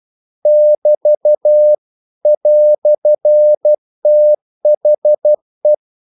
Morse code, heard as a keyed tone, about 12 wpm, 600 Hz. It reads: <BT> "THE
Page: English quotations in CW (telegraphy - morse)